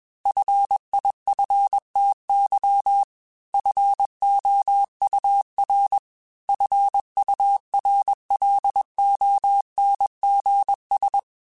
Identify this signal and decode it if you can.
Morse code, heard as a keyed tone, about 21 wpm, 800 Hz. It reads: FIFTY FOUR FURLONGS
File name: F Runs / F Runs mp3s / F Runs @20